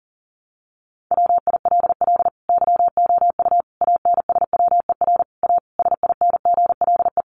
Morse code, heard as a keyed tone, about 40 wpm, 700 Hz. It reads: WILL YOU ANSWER A SINGLE
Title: Sentences from Top 700 Words 40wpm